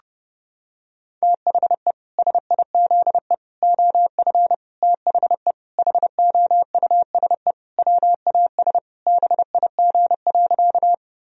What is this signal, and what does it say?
Morse code, heard as a keyed tone, about 30 wpm, 700 Hz. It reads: THE SIZE OF THE HOUSE WAS BIG.
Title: Sentences from Top 1000 Words - Review Part 2 - 30wpm